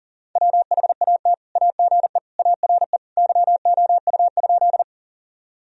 Morse code, heard as a keyed tone, about 40 wpm, 700 Hz. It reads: WHAT AGE ARE YOU?
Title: Sentences from Top 1000 Words - Review Part 3 - 40wpm